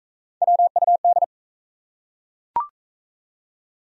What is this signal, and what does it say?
Morse code, heard as a keyed tone, about 37 wpm, 700 Hz. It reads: WUD